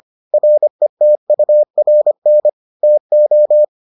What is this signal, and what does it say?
Morse code, heard as a keyed tone, about 25 wpm, 600 Hz. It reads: RETURN TO